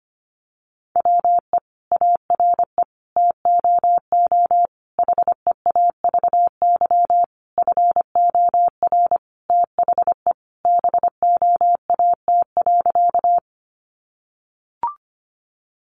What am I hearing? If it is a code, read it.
Morse code, heard as a keyed tone, about 25 wpm, 700 Hz. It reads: WE ARE TOO HEAVY FOR THE BOAT.
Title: Sentences from Top 1100 Words 25wpm